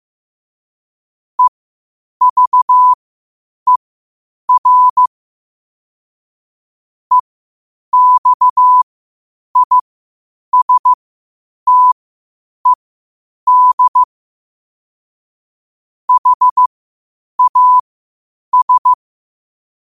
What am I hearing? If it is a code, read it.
Morse code, heard as a keyed tone, about 15 wpm, spaced as for 8 wpm, 1000 Hz. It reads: EVER EXISTED HAS
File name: qotdfiles / 8wpm